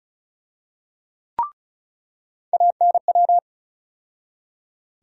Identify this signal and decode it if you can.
Morse code, heard as a keyed tone, about 34 wpm, 700 Hz. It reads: ANW
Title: USA Call Sign Postfixes - 3 Letters 35wpm